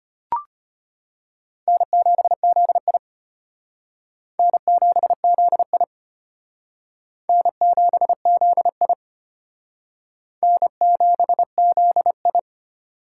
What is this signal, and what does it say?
Morse code, heard as a keyed tone, about 32 wpm, 700 Hz. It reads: N7ZI N7ZI N7ZI N7ZI